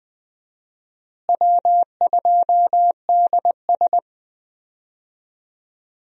Morse code, heard as a keyed tone, about 21 wpm, 700 Hz. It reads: W2DS
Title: USA Call Signs - 4 Characters - Repeated 3x - 20wpm